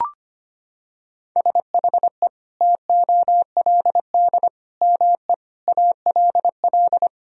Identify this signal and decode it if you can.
Morse code, heard as a keyed tone, about 25 wpm, 700 Hz. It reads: SHE TOLD ME ALL
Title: Sentences from Top 400 Words and Abbreviations 25wpm